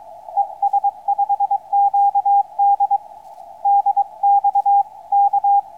Morse code, heard as a keyed tone, about 23 wpm, 800 Hz, with radio noise. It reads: ES5QD DX K
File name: es5qd 21-10-13 2048 160m